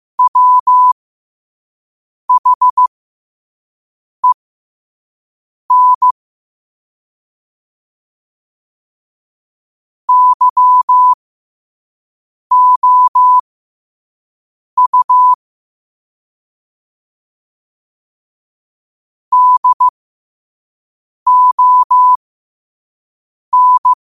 Morse code, heard as a keyed tone, about 15 wpm, spaced as for 5 wpm, 1000 Hz. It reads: WHEN YOU DON
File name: qotdfiles / 5wpm